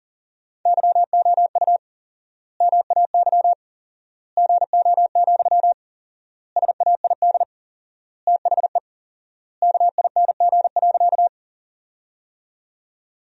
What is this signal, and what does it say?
Morse code, heard as a keyed tone, about 40 wpm, 700 Hz. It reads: YOU MAY GO, SAID THE KING.
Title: 4x Word-Spacing - Sentences from Top 400 Words - 40wpm